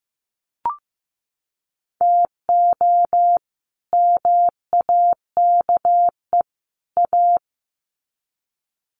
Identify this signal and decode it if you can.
Morse code, heard as a keyed tone, about 15 wpm, 700 Hz. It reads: TO MAKE A
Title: Sets of 3 Words 15wpm